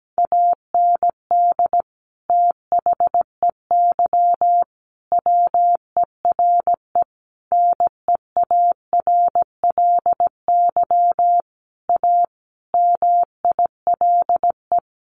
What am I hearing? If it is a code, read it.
Morse code, heard as a keyed tone, about 17 wpm, 700 Hz. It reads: AND THEY WERE NEARLY A MILE
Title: Sentences from Top 1300 Words 17wpm